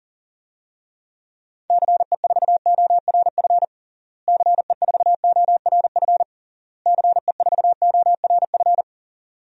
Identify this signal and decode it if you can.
Morse code, heard as a keyed tone, about 40 wpm, 700 Hz. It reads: CE4ORF CE4ORF CE4ORF
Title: DXCC-100 Call Signs - 6 Characters - Repeated 3x - 40wpm